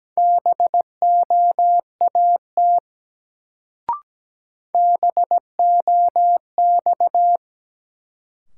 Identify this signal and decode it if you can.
Morse code, heard as a keyed tone, about 17 wpm, 700 Hz. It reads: BOAT BOX
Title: Sentences from Top 500 Words - New Word Drill - 17wpm